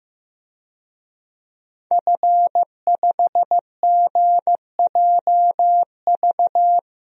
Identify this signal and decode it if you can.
Morse code, heard as a keyed tone, about 14 wpm, 700 Hz. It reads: F5GJV
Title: DXCC-100 Call Signs - 5 Characters - Repeated 3x - 15wpm